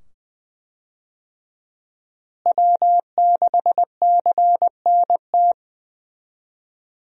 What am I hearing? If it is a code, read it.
Morse code, heard as a keyed tone, about 20 wpm, 700 Hz. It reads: W6CNT
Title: USA Call Signs - 5 Characters - Repeated 2x - 20wpm